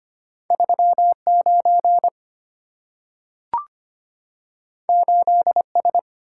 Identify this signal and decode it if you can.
Morse code, heard as a keyed tone, about 25 wpm, 700 Hz. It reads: 39 8S
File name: Two Letter-Number Combinations - 25wpm